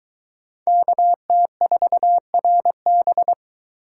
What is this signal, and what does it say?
Morse code, heard as a keyed tone, about 22 wpm, 700 Hz. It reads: KT4RB